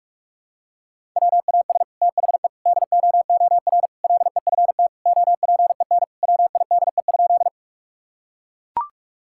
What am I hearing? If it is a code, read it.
Morse code, heard as a keyed tone, about 41 wpm, 700 Hz. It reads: WAS THE DOOR LEFT OPEN WIDE?
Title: Sentences from Top 1000 Words - Review Part 6 - 45wpm